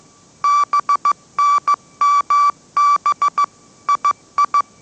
Morse code, heard as a keyed tone, about 15 wpm, 1160 Hz. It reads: BNMB II